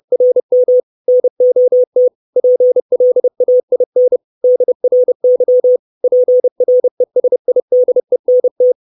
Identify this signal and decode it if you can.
Morse code, heard as a keyed tone, about 30 wpm, 500 Hz. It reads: RM NOT PLAIN DRY PRESIDENT
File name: 632 Words 30WPM